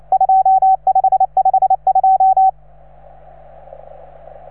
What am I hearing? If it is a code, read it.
Morse code, heard as a keyed tone, about 28 wpm, 750 Hz. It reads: 2552